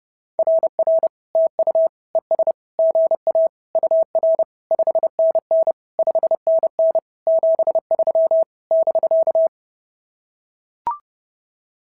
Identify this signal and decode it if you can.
Morse code, heard as a keyed tone, about 30 wpm, 690 Hz. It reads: RR TU ES GA UR 5NN 5NN 73 <BK>
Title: SOTA QSO Practice 30wpm